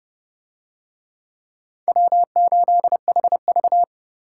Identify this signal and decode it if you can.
Morse code, heard as a keyed tone, about 31 wpm, 700 Hz. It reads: W8HV